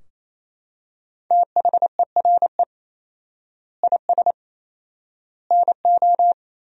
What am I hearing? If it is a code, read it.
Morse code, heard as a keyed tone, about 28 wpm, 700 Hz. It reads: THERE IS NO
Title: 4x Word-Spacing - Sentences from Top 500 Words - 28wpm